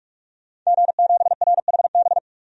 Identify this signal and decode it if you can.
Morse code, heard as a keyed tone, about 46 wpm, 700 Hz. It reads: G7RHB
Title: DXCC-100 Call Signs - 5 Characters - 45wpm